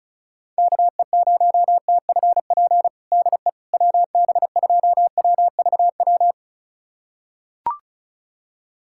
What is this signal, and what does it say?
Morse code, heard as a keyed tone, about 36 wpm, 700 Hz. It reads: KE0TFP DE WB2WVW
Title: USA Callsign DE Callsign - 6 Characters - 35wpm